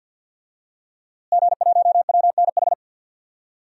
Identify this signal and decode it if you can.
Morse code, heard as a keyed tone, about 52 wpm, 700 Hz. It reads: G1WNH
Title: DXCC-100 Call Signs - 5 Characters - Repeated 3x - 50wpm